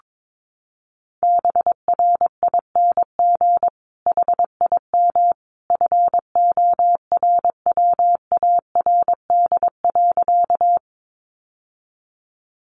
Morse code, heard as a keyed tone, about 22 wpm, 700 Hz. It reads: BRING HIM FORWARD.
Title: Sentences from Top 1300 Words 22wpm